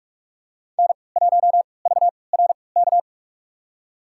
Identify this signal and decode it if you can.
Morse code, heard as a keyed tone, about 43 wpm, spaced as for 25 wpm, 700 Hz. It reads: N1VRK
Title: USA Call Signs - 5 Characters - 3x Character Spacing - 45wpm